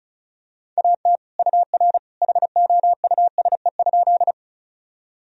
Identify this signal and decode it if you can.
Morse code, heard as a keyed tone, about 36 wpm, 700 Hz. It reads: AT UR HOUSE?
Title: Sentences from Top 200 Words and Abbreviations 35wpm